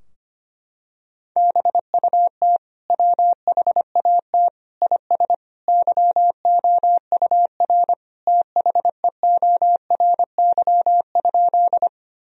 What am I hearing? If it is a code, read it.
Morse code, heard as a keyed tone, about 25 wpm, 700 Hz. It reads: BUT WHAT IS YOUR THEORY?
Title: Sentences from Top 1000 Words - Review Part 3 - 25wpm